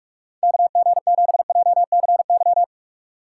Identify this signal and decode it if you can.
Morse code, heard as a keyed tone, about 41 wpm, 700 Hz. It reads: KG7JCY